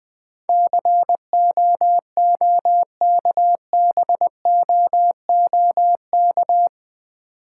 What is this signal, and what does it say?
Morse code, heard as a keyed tone, about 20 wpm, 690 Hz. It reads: COOKBOOK
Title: Difficult Words with Hints - 20wpm